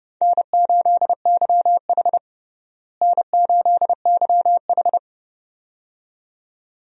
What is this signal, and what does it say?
Morse code, heard as a keyed tone, about 32 wpm, 700 Hz. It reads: N8YH N8YH
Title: USA Call Signs - 4 Characters - Repeated 2x - 30wpm